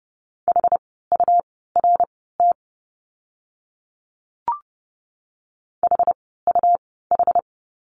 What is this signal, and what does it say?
Morse code, heard as a keyed tone, about 30 wpm, spaced as for 17 wpm, 700 Hz. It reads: HURT HUH
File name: Lesson 17f - Letter u - Words 30wpm